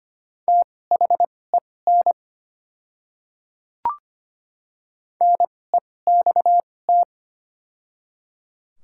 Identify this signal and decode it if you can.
Morse code, heard as a keyed tone, about 25 wpm, spaced as for 18 wpm, 700 Hz. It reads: THEN NEXT
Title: 2x Character Spacing - Top 300 Words - 25wpm